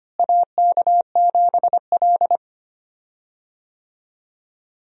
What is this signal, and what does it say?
Morse code, heard as a keyed tone, about 25 wpm, 700 Hz. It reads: AK7L
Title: USA Call Signs - 4 Characters - Repeated 3x - 25wpm